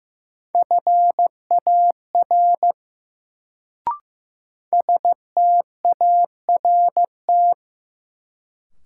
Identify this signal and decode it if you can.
Morse code, heard as a keyed tone, about 15 wpm, 700 Hz. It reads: FAR START
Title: Sentences from Top 300 Words - New Word Drill - 15wpm